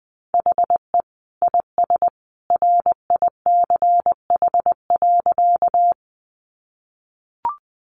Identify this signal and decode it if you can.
Morse code, heard as a keyed tone, about 20 wpm, 700 Hz. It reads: HE IS RICH.
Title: Sentences from Top 1100 Words 20wpm